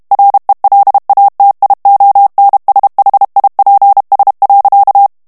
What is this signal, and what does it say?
Morse code, heard as a keyed tone, about 32 wpm, 800 Hz. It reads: RELATIONSHIPS.
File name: morse mp3 / ANGLAIS texte clair 1 vitesse 30 mots minute